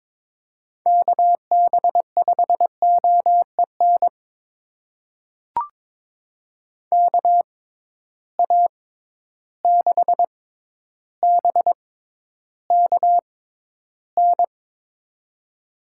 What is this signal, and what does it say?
Morse code, heard as a keyed tone, about 22 wpm, spaced as for 11 wpm, 700 Hz. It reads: KB5OEN KA6BKN
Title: USA Call Signs - 6 Characters - 6x Character Spacing - 22wpm